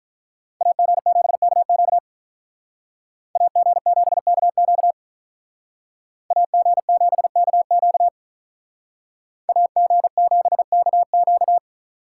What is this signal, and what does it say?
Morse code, heard as a keyed tone, about 43 wpm, 700 Hz. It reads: AG7KQ AG7KQ AG7KQ AG7KQ
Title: Speed Racing - USA 2x2 Call Signs 35wpm